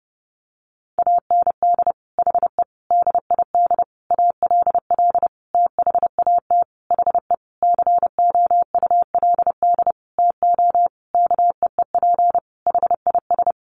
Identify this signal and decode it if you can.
Morse code, heard as a keyed tone, about 30 wpm, 700 Hz. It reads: AND HE DID ALL THAT HE COULD TO KEEP HIS
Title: Sentences from Top 1500 Words 30wpm